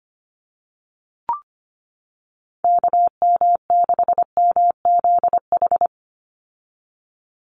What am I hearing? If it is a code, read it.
Morse code, heard as a keyed tone, about 25 wpm, 700 Hz. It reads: KM6MZH
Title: USA Call Signs 25wpm